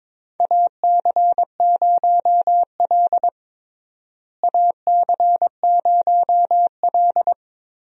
Morse code, heard as a keyed tone, about 22 wpm, 700 Hz. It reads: AC0L AC0L
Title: USA Call Signs - 4 Characters - Repeated 3x - 22wpm